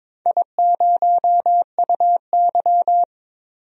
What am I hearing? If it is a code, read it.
Morse code, heard as a keyed tone, about 22 wpm, 700 Hz. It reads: I0UY
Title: DXCC-100 Call Signs - 4 Characters - 22wpm